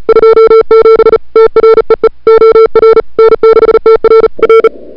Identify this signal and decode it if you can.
Morse code, heard as a keyed tone, about 37 wpm, 440 Hz. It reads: 27 TREE OR N6TR R
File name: n6tr / Fab2007CWSprintHour1